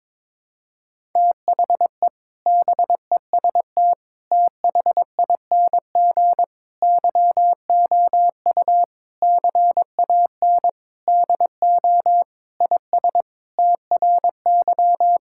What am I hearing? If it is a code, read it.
Morse code, heard as a keyed tone, about 22 wpm, 700 Hz. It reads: THE BEST THING YOU CAN DO IS TRY
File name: Sentences with 7-Word Phrases - 22wpm